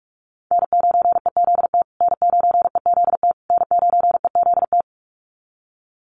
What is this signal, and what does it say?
Morse code, heard as a keyed tone, about 44 wpm, 700 Hz. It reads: N9EZT N9EZT N9EZT
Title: USA Call Signs Repeated 3x 45wpm